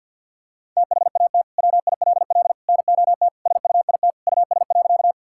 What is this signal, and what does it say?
Morse code, heard as a keyed tone, about 50 wpm, 700 Hz. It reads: THAT WILL NOT SUIT US.
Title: Sentences from Top 1600 Words 50wpm